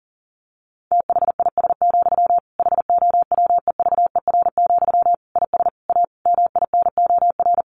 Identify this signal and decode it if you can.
Morse code, heard as a keyed tone, about 37 wpm, 700 Hz. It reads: THIS, HOWEVER, IS A MINOR
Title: Sentences from Top 1500 Words 40wpm